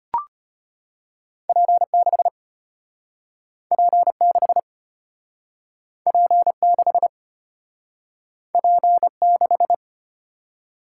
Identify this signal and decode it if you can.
Morse code, heard as a keyed tone, about 34 wpm, 700 Hz. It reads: P6 P6 P6 P6